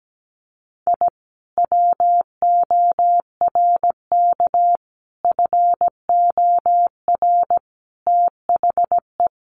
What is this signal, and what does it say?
Morse code, heard as a keyed tone, about 17 wpm, 700 Hz. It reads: I WORK FOR THE
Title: Sentences from Top 600 Words 17wpm